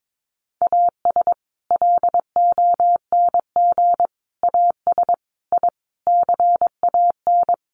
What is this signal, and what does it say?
Morse code, heard as a keyed tone, about 22 wpm, 700 Hz. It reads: AS LONG AS I CAN
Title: Sets of 5 Words Encore2 22wpm